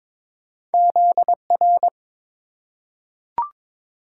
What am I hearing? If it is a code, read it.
Morse code, heard as a keyed tone, about 23 wpm, 700 Hz. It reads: ZR
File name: practice / 22wpm / Two Letter-Number Combinations - 22wpm